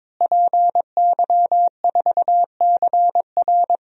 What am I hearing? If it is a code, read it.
Morse code, heard as a keyed tone, about 22 wpm, 700 Hz. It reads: PY4CR
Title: DXCC-100 Call Signs - 5 Characters - Repeated 2x - 22wpm